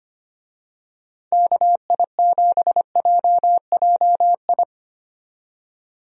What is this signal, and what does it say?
Morse code, heard as a keyed tone, about 25 wpm, 700 Hz. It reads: KI7JJI
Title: USA Call Signs - 6 Characters - 25wpm